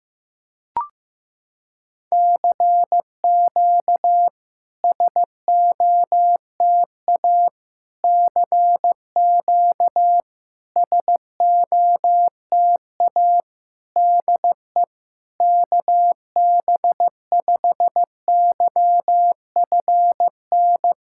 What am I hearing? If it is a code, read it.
Morse code, heard as a keyed tone, about 15 wpm, 700 Hz. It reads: CQ SOTA CQ SOTA DE KB5YFN
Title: SOTA QSO Practice 15wpm